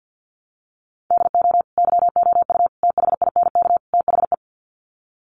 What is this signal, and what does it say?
Morse code, heard as a keyed tone, about 50 wpm, 700 Hz. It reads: DO YOU THINK THE
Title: Sets of 4 Words 50wpm